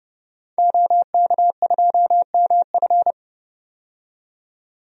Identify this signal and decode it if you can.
Morse code, heard as a keyed tone, about 30 wpm, 700 Hz. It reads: OK2MF
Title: DXCC-100 Call Signs - 5 Characters - 30wpm